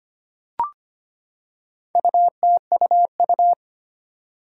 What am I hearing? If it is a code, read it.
Morse code, heard as a keyed tone, about 25 wpm, 700 Hz. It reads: UTUU
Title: Four Challenging Letter-Number Combinations 25wpm